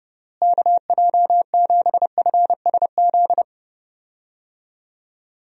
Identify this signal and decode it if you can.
Morse code, heard as a keyed tone, about 30 wpm, 700 Hz. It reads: KJ7FSZ
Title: USA Call Signs - 6 Characters - Repeated 2x - 30wpm